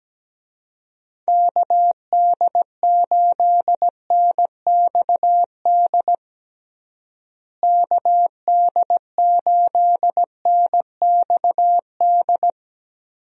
Morse code, heard as a keyed tone, about 17 wpm, 700 Hz. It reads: KD8NXD KD8NXD
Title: USA Call Signs - 6 Characters - Repeated 3x - 17wpm